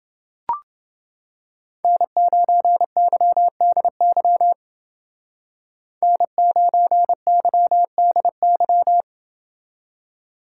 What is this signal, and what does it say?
Morse code, heard as a keyed tone, about 29 wpm, 700 Hz. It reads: N9YDY N9YDY